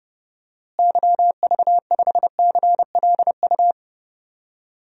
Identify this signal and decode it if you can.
Morse code, heard as a keyed tone, about 32 wpm, 700 Hz. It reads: YV5CLU
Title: DXCC-100 Call Signs - 6 Characters - Repeated 2x - 30wpm